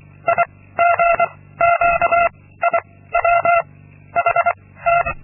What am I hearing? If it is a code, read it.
Morse code, heard as a keyed tone, about 25 wpm, 680 Hz. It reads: I G Q I W H T
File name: Linksmakalnis-2